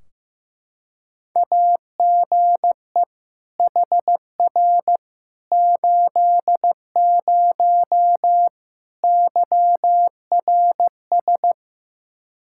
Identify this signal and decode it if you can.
Morse code, heard as a keyed tone, about 15 wpm, 700 Hz. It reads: AGE HR 80 YRS
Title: K1USN SST QSO Practice 15wpm